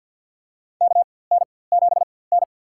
Morse code, heard as a keyed tone, about 52 wpm, spaced as for 24 wpm, 690 Hz. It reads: KN7N
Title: USA Call Signs - 4x Character Spacing - 50wpm